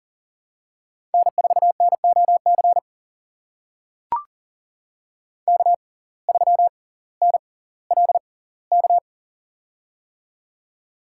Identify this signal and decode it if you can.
Morse code, heard as a keyed tone, about 38 wpm, spaced as for 20 wpm, 700 Hz. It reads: N4NOC K3NLK
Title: USA Call Signs - 5 Characters - 6x Character Spacing - 40wpm